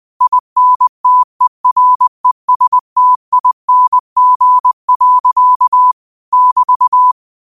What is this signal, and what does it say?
Morse code, heard as a keyed tone, about 20 wpm, 1000 Hz. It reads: INTERESTING. <BT>
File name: qotdfiles / 20wpm